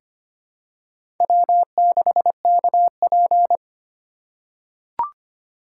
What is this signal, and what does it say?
Morse code, heard as a keyed tone, about 25 wpm, 700 Hz. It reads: W6KP
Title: USA Call Signs - 4 Characters - 3x Character Spacing - 25wpm